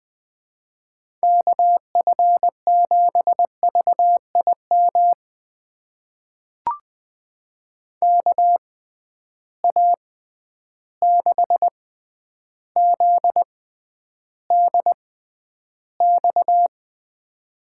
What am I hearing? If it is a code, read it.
Morse code, heard as a keyed tone, about 21 wpm, spaced as for 10 wpm, 700 Hz. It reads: KF7VIM KA6ZDX
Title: USA Call Signs - 6 Characters - 6x Character Spacing - 20wpm